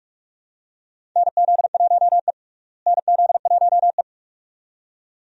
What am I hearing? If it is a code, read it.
Morse code, heard as a keyed tone, about 46 wpm, 700 Hz. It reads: NZ1E NZ1E
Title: USA Call Signs - 4 Characters - Repeated 2x - 45wpm